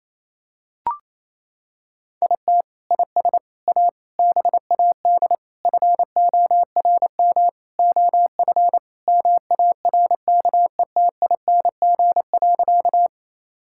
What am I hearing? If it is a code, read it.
Morse code, heard as a keyed tone, about 28 wpm, 700 Hz. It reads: IT IS A BAD FORM OF MARKETING.
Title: Sentences from Top 1000 Words - Review Part 5 - 28wpm